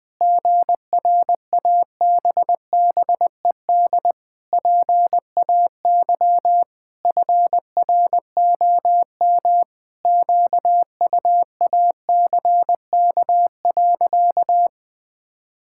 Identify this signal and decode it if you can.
Morse code, heard as a keyed tone, about 20 wpm, 700 Hz. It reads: GRABBED PAY FROM QUACK.
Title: Pangram Challenge v2 - 20wpm